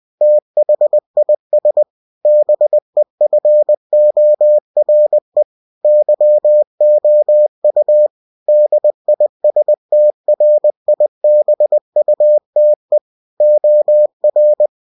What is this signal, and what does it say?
Morse code, heard as a keyed tone, about 20 wpm, 600 Hz. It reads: THIS BEFORE YOU DISTRIBUTE OR
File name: cw-books / winnie-the-pooh / winnie-20wpm-page-0112